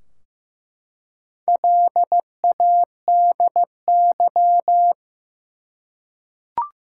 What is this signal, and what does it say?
Morse code, heard as a keyed tone, about 15 wpm, 700 Hz. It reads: LADY
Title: Four Letter Words - 15wpm